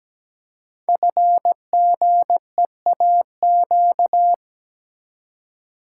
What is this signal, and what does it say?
Morse code, heard as a keyed tone, about 17 wpm, 700 Hz. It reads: FGEAQ
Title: Code Groups - 5 Letters - 17wpm